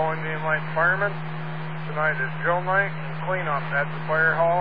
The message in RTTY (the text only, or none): none